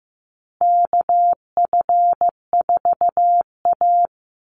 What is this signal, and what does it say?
Morse code, heard as a keyed tone, about 15 wpm, 700 Hz. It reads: KF4A
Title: Call Signs Repeated 2x - 15wpm